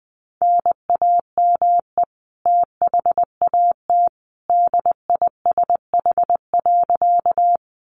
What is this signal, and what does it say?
Morse code, heard as a keyed tone, about 20 wpm, 700 Hz. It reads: NAME THAT DISH.
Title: Sentences from Top 1300 Words 20wpm